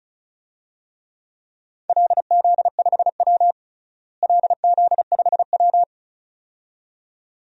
Morse code, heard as a keyed tone, about 34 wpm, 700 Hz. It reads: LZ5W LZ5W
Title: DXCC-100 Call Signs - 4 Characters - Repeated 2x - 35wpm